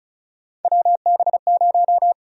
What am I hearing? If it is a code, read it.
Morse code, heard as a keyed tone, about 36 wpm, 700 Hz. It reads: WB0
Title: USA Call Sign Prefixes - 35wpm